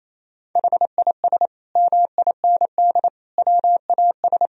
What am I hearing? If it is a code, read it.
Morse code, heard as a keyed tone, about 28 wpm, 700 Hz. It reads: HIS MIND WAS
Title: Sentences from Top 1000 Words - Review Part 2 - 28wpm